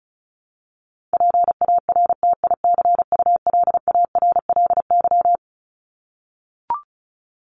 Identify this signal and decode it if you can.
Morse code, heard as a keyed tone, about 34 wpm, 700 Hz. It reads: PARTICULARLY
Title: LY Suffix - Top 100 Words with Base Words 35wpm